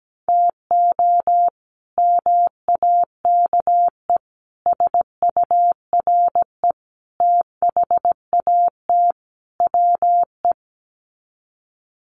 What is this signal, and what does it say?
Morse code, heard as a keyed tone, about 17 wpm, 700 Hz. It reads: TO MAKE SURE THAT WE
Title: Sets of 5 Words Encore 17wpm